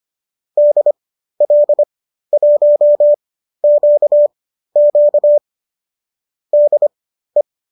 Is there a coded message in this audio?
Yes: Morse code from 0.6 to 7.4 s, heard as a keyed tone, about 25 wpm, spaced as for 13 wpm, 590 Hz. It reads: DL1QQ DE